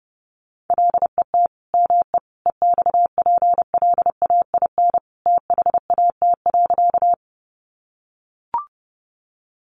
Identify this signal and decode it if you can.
Morse code, heard as a keyed tone, about 30 wpm, 700 Hz. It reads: LET ME EXPLAIN THAT.
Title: Sentences from Top 1100 Words 30wpm